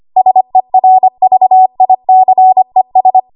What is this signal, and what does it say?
Morse code, heard as a keyed tone, about 25 wpm, 750 Hz. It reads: SERVICES
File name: Archive / 25 WPM / 25WPM